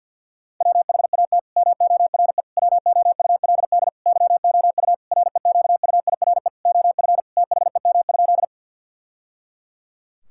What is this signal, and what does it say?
Morse code, heard as a keyed tone, about 52 wpm, 700 Hz. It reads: WHAT MORE WOULD YOU REQUIRE OF THEM?
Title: Sentences from Top 1000 Words - Review Part 6 - 50wpm